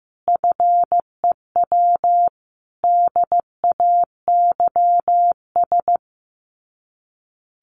Morse code, heard as a keyed tone, about 15 wpm, 700 Hz. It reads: FEW DAYS
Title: Sets of 5 Words Encore2 15wpm